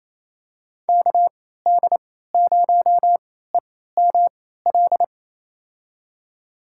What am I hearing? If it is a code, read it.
Morse code, heard as a keyed tone, about 28 wpm, spaced as for 16 wpm, 700 Hz. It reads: KD0EML
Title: USA Call Signs - 3x Character Spacing - 28wpm